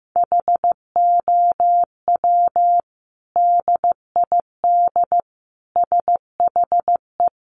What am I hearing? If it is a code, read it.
Morse code, heard as a keyed tone, about 15 wpm, 700 Hz. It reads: HOW DID SHE
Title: Sentences from Top 1300 Words 15wpm